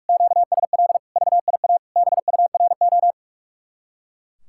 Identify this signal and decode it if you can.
Morse code, heard as a keyed tone, about 43 wpm, 700 Hz. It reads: QSL VIA BURO
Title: Speed-Racing Common QSO Phrases Encore - 45wpm